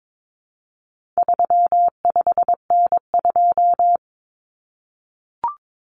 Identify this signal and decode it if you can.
Morse code, heard as a keyed tone, about 22 wpm, 700 Hz. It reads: 35N2
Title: Common Cut Numbers - 4-Digit 22wpm